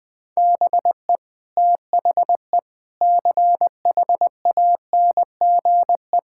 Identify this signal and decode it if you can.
Morse code, heard as a keyed tone, about 20 wpm, 700 Hz. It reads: BE THE CHANGE
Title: Sentences with 8-Word Phrases - Spoken Beginning - 20wpm